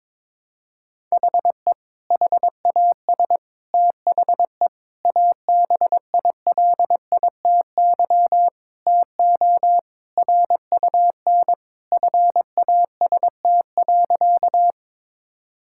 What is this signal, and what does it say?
Morse code, heard as a keyed tone, about 22 wpm, 700 Hz. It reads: HE HAS THE ABILITY TO RUN FAST.
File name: practice / 22wpm / Sentences with 3-Word Phrases - 22wpm